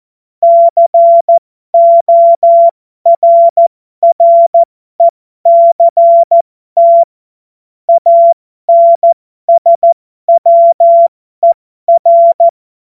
Morse code, heard as a keyed tone, about 14 wpm, spaced as for 12 wpm, 680 Hz. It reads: CORRECT ANSWER